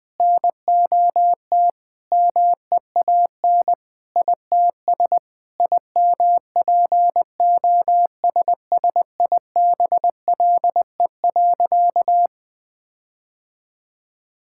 Morse code, heard as a keyed tone, about 21 wpm, 700 Hz. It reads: NOT MEAN ITS IMPOSSIBLE.
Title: Sentences with 5-Word Phrases - 20wpm